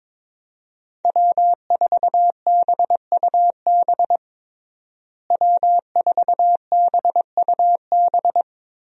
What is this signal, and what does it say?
Morse code, heard as a keyed tone, about 22 wpm, 700 Hz. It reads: W4BUB W4BUB
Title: USA Call Signs - 5 Characters - Repeated 3x - 22wpm